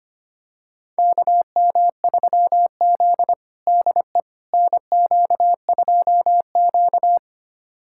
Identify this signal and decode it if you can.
Morse code, heard as a keyed tone, about 25 wpm, 700 Hz. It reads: KM3Z DE NQ2Q
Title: USA Callsign DE Callsign - 4 Characters - 25wpm